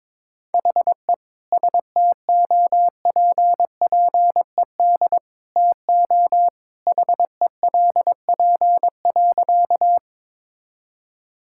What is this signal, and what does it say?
Morse code, heard as a keyed tone, about 22 wpm, 700 Hz. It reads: HE STOPPED TO HELP.
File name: practice / 22wpm / Sentences with 2-Word Phrases - 22wpm